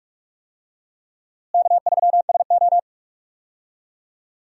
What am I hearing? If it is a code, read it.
Morse code, heard as a keyed tone, about 44 wpm, 700 Hz. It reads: K3SO
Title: USA Call Signs - 4 Characters - Repeated 3x - 45wpm